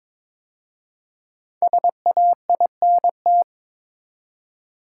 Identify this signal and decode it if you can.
Morse code, heard as a keyed tone, about 22 wpm, 700 Hz. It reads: SAINT
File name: 5 Letter Words - First 3 Letters Given - 22wpm